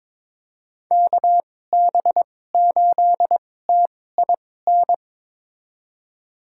Morse code, heard as a keyed tone, about 22 wpm, spaced as for 16 wpm, 700 Hz. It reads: KB8TIN
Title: USA Call Signs - 6 Characters - 2x Character Spacing - 22wpm